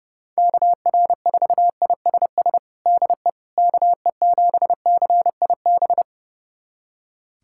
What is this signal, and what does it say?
Morse code, heard as a keyed tone, about 30 wpm, 690 Hz. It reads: KR4ISS DE KE7CIB
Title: USA Callsign DE Callsign - 6 Characters - 30wpm